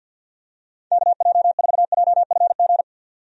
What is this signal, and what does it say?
Morse code, heard as a keyed tone, about 51 wpm, 700 Hz. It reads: KJ4JFG
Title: USA Call Signs - 6 Characters - 50wpm